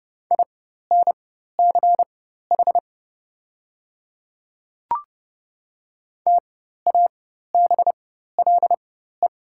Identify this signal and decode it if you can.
Morse code, heard as a keyed tone, about 30 wpm, spaced as for 14 wpm, 700 Hz. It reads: INCH TABLE
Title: 4x Character Spacing - Top 500 Words - 30wpm